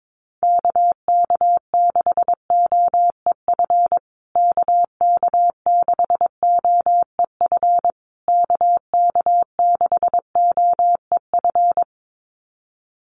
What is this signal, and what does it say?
Morse code, heard as a keyed tone, about 22 wpm, 700 Hz. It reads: KK6OEF KK6OEF KK6OEF
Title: USA Call Signs Repeated 3x 22wpm